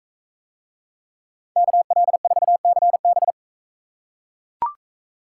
Morse code, heard as a keyed tone, about 39 wpm, 700 Hz. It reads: KL4CB